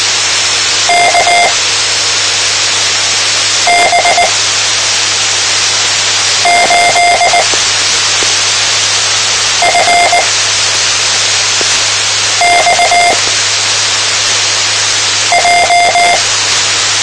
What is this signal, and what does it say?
Morse code, heard as a keyed tone, about 19 wpm, spaced as for 4 wpm, 700 Hz, with radio noise. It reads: KB8FXJ
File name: tmranets / Elmer 04 19 2015